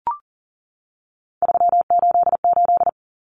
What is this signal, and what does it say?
Morse code, heard as a keyed tone, about 40 wpm, 700 Hz. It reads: 388
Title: Common Cut Numbers - 3-Digit 40wpm